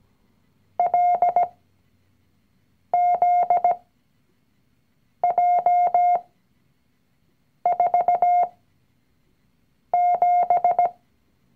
Morse code, heard as a keyed tone, about 17 wpm, spaced as for 5 wpm, 710 Hz. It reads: LZJ47
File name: CW-Online-Kurs / Anfänger / Lektion 019